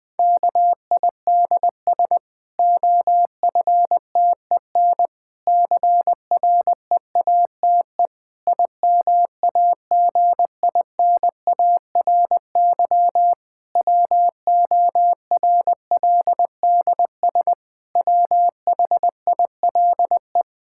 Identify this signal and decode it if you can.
Morse code, heard as a keyed tone, about 20 wpm, 700 Hz. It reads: KIDS OFTEN CREATE IMAGINARY WORLDS WHILE
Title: Difficult Sentences with Hints - 20wpm